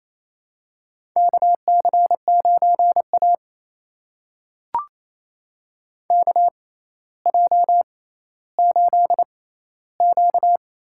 Morse code, heard as a keyed tone, about 28 wpm, spaced as for 14 wpm, 700 Hz. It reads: KC9A KJ8Q
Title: USA Call Signs - 4 Characters - 6x Character Spacing - 28wpm